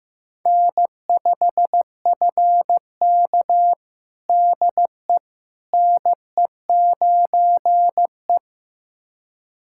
Morse code, heard as a keyed tone, about 16 wpm, 700 Hz. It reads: N5FK DE NE9E
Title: USA Callsign DE Callsign - 4 Characters - 15wpm